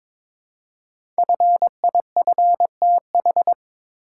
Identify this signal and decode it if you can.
Morse code, heard as a keyed tone, about 22 wpm, 690 Hz. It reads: FIFTH
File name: 5 Letter Words - First 3 Letters Given - 22wpm